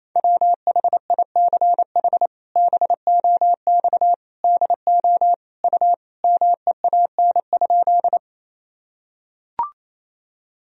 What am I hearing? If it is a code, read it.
Morse code, heard as a keyed tone, about 28 wpm, 700 Hz. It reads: WHICH BOX DO U MEAN?
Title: Sentences from Top 500 Words and Abbreviations 28wpm